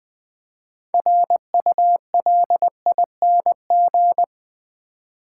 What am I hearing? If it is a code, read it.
Morse code, heard as a keyed tone, about 20 wpm, 700 Hz. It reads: RULING
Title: Build A Word - 3 Letters At A Time - 6-Letter Words - 20wpm